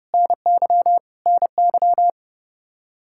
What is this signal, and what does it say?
Morse code, heard as a keyed tone, about 30 wpm, 700 Hz. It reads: NY NY